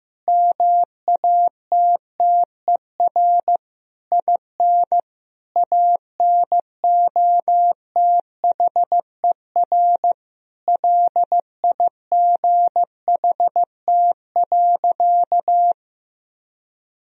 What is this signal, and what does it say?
Morse code, heard as a keyed tone, about 15 wpm, 700 Hz. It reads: MATTER IN ANOTHER LIGHT.
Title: Sentences from Top 1000 Words - Review Part 1 - 15wpm